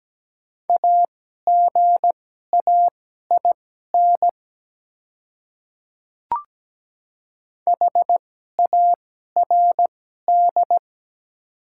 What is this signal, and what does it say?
Morse code, heard as a keyed tone, about 17 wpm, spaced as for 12 wpm, 690 Hz. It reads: AGAIN HARD
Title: 2x Character Spacing - Top 300 Words - 17wpm